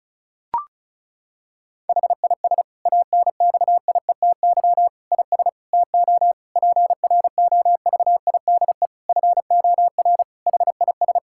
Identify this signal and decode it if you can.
Morse code, heard as a keyed tone, about 34 wpm, 700 Hz. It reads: HIS ANXIETY IS TO PROVIDE FOR HIS
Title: Sentences from Top 1600 Words 35wpm